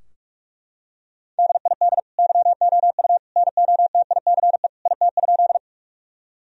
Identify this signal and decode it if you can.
Morse code, heard as a keyed tone, about 41 wpm, 700 Hz. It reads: DID YOU NOTICE IT?
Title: Sentences from Top 1000 Words - Review Part 2 - 45wpm